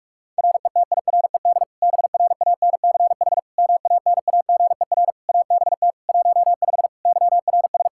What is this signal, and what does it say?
Morse code, heard as a keyed tone, about 46 wpm, 700 Hz. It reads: RETIRED BRANCH MANAGER ABT 15 YRS
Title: Practice QSO - Exchange 3 - More Info - 45wpm